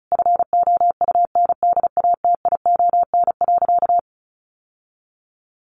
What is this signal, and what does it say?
Morse code, heard as a keyed tone, about 34 wpm, 700 Hz. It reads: FOUNDATION.
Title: Sentences from Top 1200 Words 35wpm